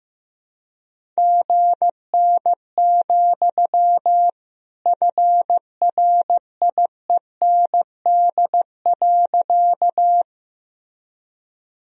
Morse code, heard as a keyed tone, about 15 wpm, 700 Hz. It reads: GN, FRIEND.
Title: Sentences from Top 300 Words and Abbreviations 15wpm